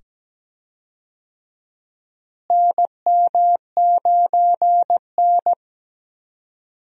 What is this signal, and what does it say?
Morse code, heard as a keyed tone, about 17 wpm, 700 Hz. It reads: NM9N
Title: USA Call Signs - 4 Characters - Repeated 2x - 17wpm